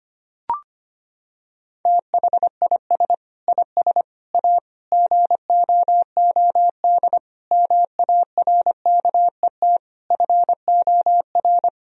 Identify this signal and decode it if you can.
Morse code, heard as a keyed tone, about 25 wpm, 700 Hz. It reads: THIS IS A GOOD MARKET FOR
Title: Sentences from Top 1000 Words - Review Part 3 - 25wpm